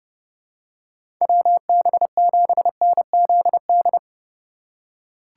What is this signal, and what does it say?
Morse code, heard as a keyed tone, about 30 wpm, 700 Hz. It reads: WB7NZD
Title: USA Call Signs - 6 Characters - 3x Character Spacing - 30wpm